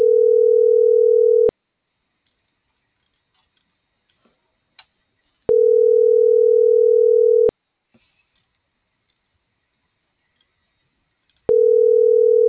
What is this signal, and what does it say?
Ringback tone (North America), 440 + 480 Hz.